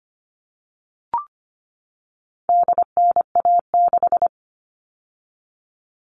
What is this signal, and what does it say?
Morse code, heard as a keyed tone, about 25 wpm, 700 Hz. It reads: DNA6
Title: All Cut Numbers - 4-Digit 25wpm